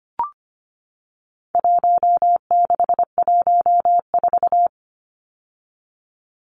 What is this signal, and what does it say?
Morse code, heard as a keyed tone, about 25 wpm, 700 Hz. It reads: 1614
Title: Numbers - Contest Style - 4-Digit 25wpm